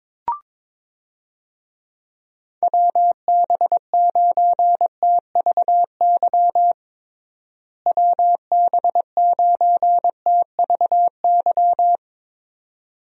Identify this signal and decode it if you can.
Morse code, heard as a keyed tone, about 22 wpm, 700 Hz. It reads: WB9TVY WB9TVY
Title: USA Call Signs - 6 Characters - Repeated 2x - 22wpm